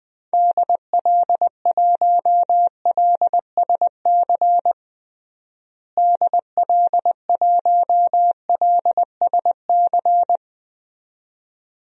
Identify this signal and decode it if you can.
Morse code, heard as a keyed tone, about 20 wpm, 700 Hz. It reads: DL1LSC DL1LSC
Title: DXCC-100 Call Signs - 6 Characters - Repeated 2x - 20wpm